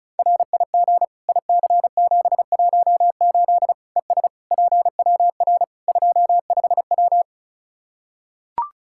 Morse code, heard as a keyed tone, about 36 wpm, 700 Hz. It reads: RIG IC718 ES PWR 25W
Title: Practice QSO - Exchange 2 - Info - 35wpm